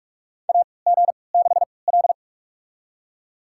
Morse code, heard as a keyed tone, about 41 wpm, spaced as for 25 wpm, 700 Hz. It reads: AG6L
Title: USA Call Signs - 4 Characters - 3x Character Spacing - 45wpm